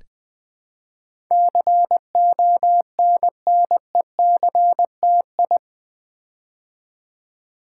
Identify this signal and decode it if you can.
Morse code, heard as a keyed tone, about 20 wpm, 700 Hz. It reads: CONNECTI
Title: Build A Word - 4 Letters At A Time - 10-Letter Words - 20wpm